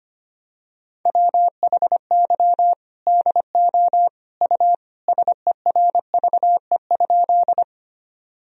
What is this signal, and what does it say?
Morse code, heard as a keyed tone, about 25 wpm, 700 Hz. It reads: WHY DO U SERVE?
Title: Sentences from Top 400 Words and Abbreviations 25wpm